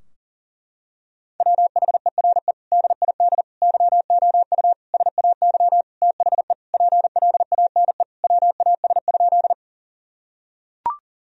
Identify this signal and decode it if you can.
Morse code, heard as a keyed tone, about 37 wpm, 700 Hz. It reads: WHERE DID YOU SAY THE PLANE WAS?
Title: Sentences from Top 1000 Words - Review Part 3 - 40wpm